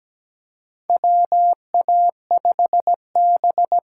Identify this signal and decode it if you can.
Morse code, heard as a keyed tone, about 17 wpm, 700 Hz. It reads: WA5B
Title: USA Call Signs - 8x Character Spacing - 17wpm